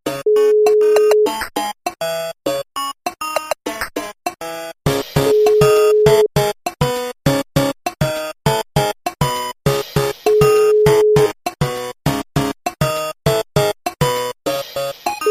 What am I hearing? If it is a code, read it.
Ringback tone (Europe, 425 Hz), with noise.